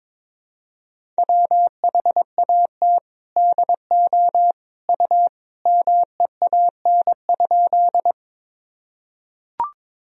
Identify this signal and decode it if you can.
Morse code, heard as a keyed tone, about 22 wpm, 700 Hz. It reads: WHAT DO U MEAN?
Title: Sentences from Top 200 Words and Abbreviations 22wpm